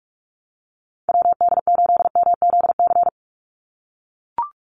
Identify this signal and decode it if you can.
Morse code, heard as a keyed tone, about 44 wpm, 700 Hz. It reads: WD8MZC
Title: USA Call Signs 45wpm